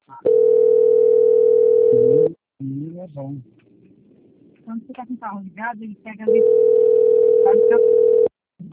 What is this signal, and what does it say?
Ringback tone (North America), 440 + 480 Hz, with speech.
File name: out- - - - - .747